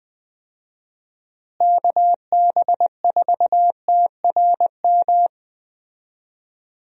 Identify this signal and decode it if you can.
Morse code, heard as a keyed tone, about 20 wpm, 700 Hz. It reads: KB4TRM
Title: USA Call Signs - 6 Characters - Repeated 2x - 20wpm